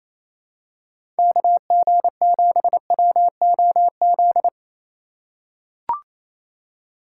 Morse code, heard as a keyed tone, about 28 wpm, 700 Hz. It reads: KG7WOZ
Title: USA Call Signs - 2x Character Spacing - 28wpm